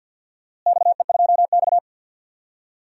Morse code, heard as a keyed tone, about 53 wpm, 700 Hz. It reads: XE2X
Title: DXCC-100 Call Signs - 4 Characters - Repeated 2x - 50wpm